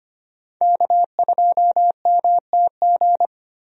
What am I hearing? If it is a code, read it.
Morse code, heard as a keyed tone, about 25 wpm, 700 Hz. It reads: K2MTG
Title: USA Call Signs - 5 Characters - Repeated 3x - 25wpm